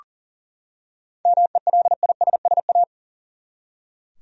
Morse code, heard as a keyed tone, about 40 wpm, 700 Hz. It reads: MELISSA
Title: USA Common Names - 40wpm